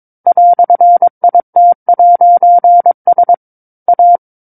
Morse code, heard as a keyed tone, about 22 wpm, 700 Hz. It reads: "IT'S A
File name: 06-Chapter-06 22 WPM